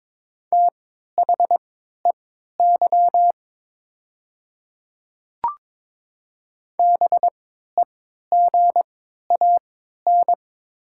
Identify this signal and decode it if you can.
Morse code, heard as a keyed tone, about 22 wpm, spaced as for 12 wpm, 690 Hz. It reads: THEY BEGAN
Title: 3x Character Spacing - Top 300 Words - 22wpm